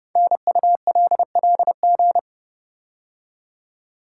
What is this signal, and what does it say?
Morse code, heard as a keyed tone, about 29 wpm, 700 Hz. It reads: NULLG
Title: Code Groups - 5 Letter:Numbers - 30wpm